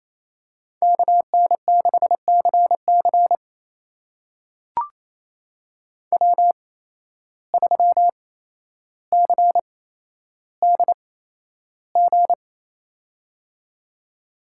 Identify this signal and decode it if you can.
Morse code, heard as a keyed tone, about 28 wpm, spaced as for 12 wpm, 700 Hz. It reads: KN6CC W3CDG
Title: USA Call Signs - 5 Characters - 8x Character Spacing - 28wpm